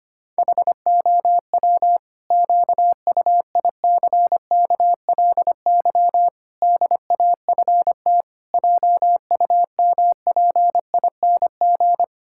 Morse code, heard as a keyed tone, about 25 wpm, 700 Hz. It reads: HOW QUICKLY DAFT JUMPING
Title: Pangram Challenge v2 - 25wpm